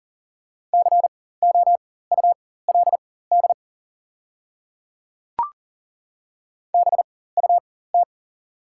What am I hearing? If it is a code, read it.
Morse code, heard as a keyed tone, about 39 wpm, spaced as for 19 wpm, 700 Hz. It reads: COULD BUT
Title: 4x Character Spacing - Top 100 Words - 40wpm